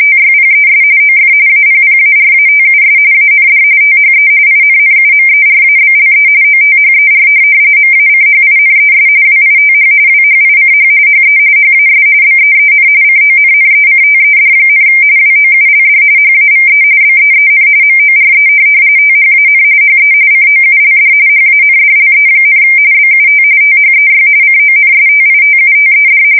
RTTY, about 45.45 baud. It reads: TYPES OF APPARATUS AND THE INNUMERABLE OTHER MACHINES WITH WHICH HE HAS BEEN CONNECTED, WOULD HAVE BEEN MORE THAN HE COULD HANDLE, BUT THEY CONSTITUTED